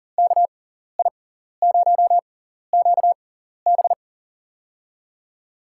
Morse code, heard as a keyed tone, about 40 wpm, spaced as for 14 wpm, 700 Hz. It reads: KI0QB
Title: USA Call Signs - 6x Character Spacing - 40wpm